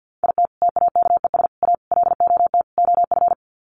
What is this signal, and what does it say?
Morse code, heard as a keyed tone, about 50 wpm, 700 Hz. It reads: IT TAKES A LOT OF
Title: Sets of 5 Words Encore 50wpm